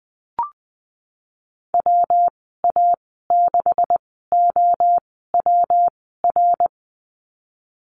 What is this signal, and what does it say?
Morse code, heard as a keyed tone, about 20 wpm, spaced as for 15 wpm, 700 Hz. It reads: WA6OWR
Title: 128 Lesson 21g - Number 6 - Call Signs 20wpm